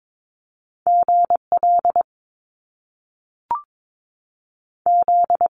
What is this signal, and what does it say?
Morse code, heard as a keyed tone, about 22 wpm, 700 Hz. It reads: GL Z
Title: Top Two Letter Combinations in Words 22wpm